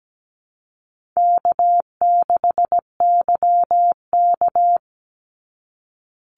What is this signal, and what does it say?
Morse code, heard as a keyed tone, about 17 wpm, 700 Hz. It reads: K6YK
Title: NA SOTA Chaser Callsigns 17wpm